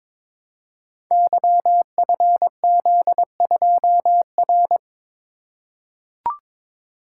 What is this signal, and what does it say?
Morse code, heard as a keyed tone, about 22 wpm, 700 Hz. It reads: YFZ2R
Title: Code Groups - 5 Letter:Numbers - 22wpm